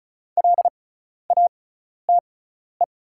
Morse code, heard as a keyed tone, about 34 wpm, spaced as for 12 wpm, 700 Hz. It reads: LATE
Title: 6x Character Spacing - Top 300 Words - 35wpm